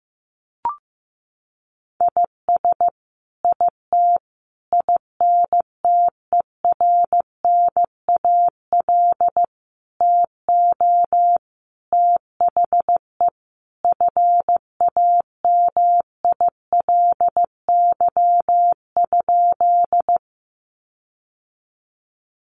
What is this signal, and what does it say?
Morse code, heard as a keyed tone, about 15 wpm, 700 Hz. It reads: IS IT INTERNAL TO THE FAMILY?
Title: Sentences from Top 1400 Words 15wpm